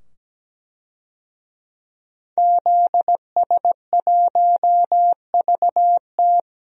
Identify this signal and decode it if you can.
Morse code, heard as a keyed tone, about 17 wpm, 700 Hz. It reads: ZS1VT
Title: DXCC-100 Call Signs - 5 Characters - 17wpm